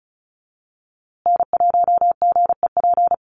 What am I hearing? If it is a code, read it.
Morse code, heard as a keyed tone, about 34 wpm, 700 Hz. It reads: N1GEP
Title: USA Call Signs Repeated 3x 35wpm